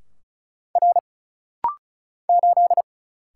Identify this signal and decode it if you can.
Morse code, heard as a keyed tone, about 35 wpm, 700 Hz. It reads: R 8
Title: Single Letter-Number - Rapid-Fire 35wpm